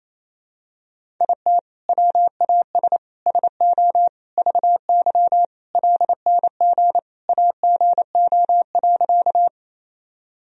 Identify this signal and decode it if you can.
Morse code, heard as a keyed tone, about 28 wpm, 700 Hz. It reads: IT WAS SO VY LNG AGO.
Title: Sentences from Top 500 Words and Abbreviations 28wpm